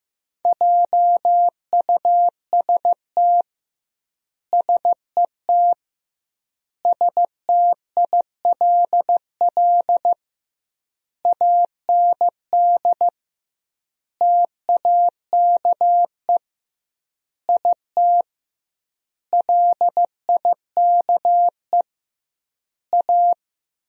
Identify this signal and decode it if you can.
Morse code, heard as a keyed tone, about 15 wpm, 700 Hz. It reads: JUST SET STILL AND TAKE IT LIKE A
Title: 2x Word-Spacing - Sentences from Top 300 Words - 15wpm